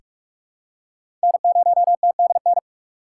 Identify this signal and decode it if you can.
Morse code, heard as a keyed tone, about 46 wpm, 700 Hz. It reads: N0TDN
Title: USA Call Signs - 8x Character Spacing - 45wpm